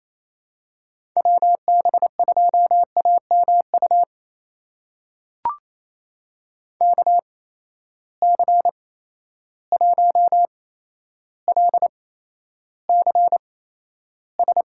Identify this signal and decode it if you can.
Morse code, heard as a keyed tone, about 28 wpm, spaced as for 12 wpm, 700 Hz. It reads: WB2AMU KC1LCS
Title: USA Call Signs - 6 Characters - 8x Character Spacing - 28wpm